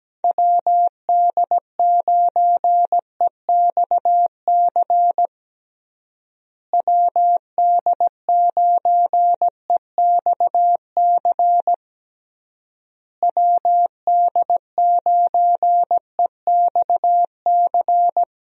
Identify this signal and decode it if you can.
Morse code, heard as a keyed tone, about 17 wpm, 700 Hz. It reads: WD9EXC WD9EXC WD9EXC
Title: USA Call Signs - 6 Characters - Repeated 3x - 17wpm